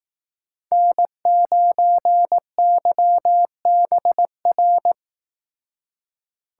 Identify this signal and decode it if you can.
Morse code, heard as a keyed tone, about 18 wpm, 700 Hz. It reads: N9YBR
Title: Speed Racing - USA 1x3 Call Signs 15wpm